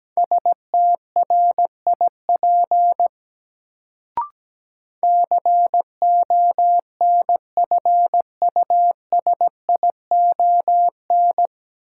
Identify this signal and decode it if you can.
Morse code, heard as a keyed tone, about 17 wpm, 700 Hz. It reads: STRIP CONFUSION
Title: Sentences from Top 1600 Words - New Word Drill Alt - 17wpm